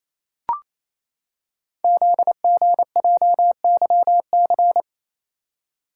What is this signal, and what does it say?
Morse code, heard as a keyed tone, about 27 wpm, 700 Hz. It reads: ZGJYC
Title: Code Groups - 5 Letter:Numbers - 28wpm